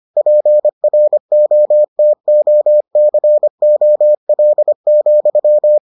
Morse code, heard as a keyed tone, about 25 wpm, 600 Hz. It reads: PROTOCOL,